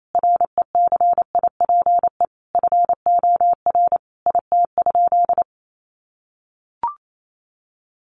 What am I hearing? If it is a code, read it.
Morse code, heard as a keyed tone, about 28 wpm, 700 Hz. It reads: RECIPE FOR IT?
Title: Sentences from Top 1200 Words 28wpm